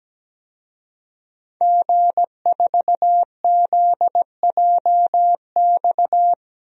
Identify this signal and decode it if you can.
Morse code, heard as a keyed tone, about 17 wpm, 700 Hz. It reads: G4ZJX
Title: DXCC-100 Call Signs - 5 Characters - Repeated 3x - 17wpm